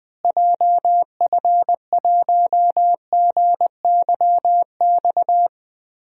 Morse code, heard as a keyed tone, about 20 wpm, 700 Hz. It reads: JF1GYX
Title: DXCC-100 Call Signs - 6 Characters - Repeated 3x - 20wpm